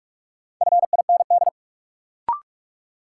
Morse code, heard as a keyed tone, about 43 wpm, 700 Hz. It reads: FIND
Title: Sentences from Top 100 Words - New Word Drill - 45wpm